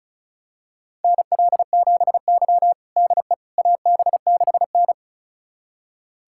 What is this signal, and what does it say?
Morse code, heard as a keyed tone, about 36 wpm, 700 Hz. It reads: NL7Y DE AB6N
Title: USA Callsign DE Callsign - 4 Characters - 35wpm